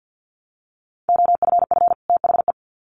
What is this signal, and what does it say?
Morse code, heard as a keyed tone, about 50 wpm, 700 Hz. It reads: OFF THE
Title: Sets of 2 Words 50wpm